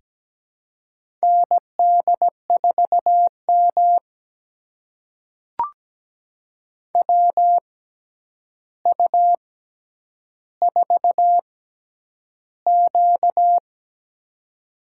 Morse code, heard as a keyed tone, about 17 wpm, spaced as for 9 wpm, 700 Hz. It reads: ND4M WU4Q
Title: USA Call Signs - 4 Characters - 6x Character Spacing - 17wpm